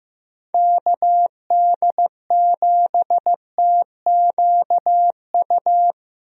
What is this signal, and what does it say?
Morse code, heard as a keyed tone, about 15 wpm, 700 Hz. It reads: KD7TQU
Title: USA Call Signs - 6 Characters - Repeated 3x - 15wpm